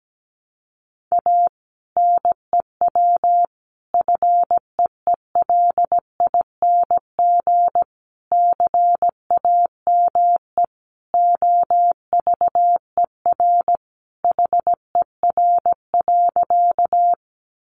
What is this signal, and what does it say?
Morse code, heard as a keyed tone, about 17 wpm, 700 Hz. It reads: A NEW FEELING CAME OVER HER.
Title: Sentences from Top 1200 Words 17wpm